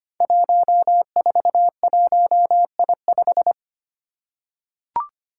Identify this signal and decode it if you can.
Morse code, heard as a keyed tone, about 25 wpm, 700 Hz. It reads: 141I5
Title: Five Challenging Letter-Number Combinations 25wpm